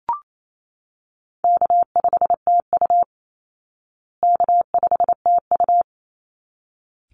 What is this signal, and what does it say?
Morse code, heard as a keyed tone, about 28 wpm, 700 Hz. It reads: K5TU K5TU
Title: Easy Call Signs Repeated 2x - 28wpm